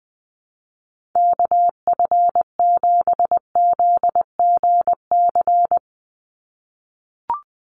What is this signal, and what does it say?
Morse code, heard as a keyed tone, about 20 wpm, 700 Hz. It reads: KF7ZGC
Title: USA Call Signs 20wpm